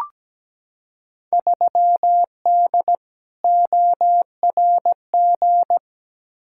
Morse code, heard as a keyed tone, about 17 wpm, 700 Hz. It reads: 3D ORG
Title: Field Day Exchanges Encore - 17wpm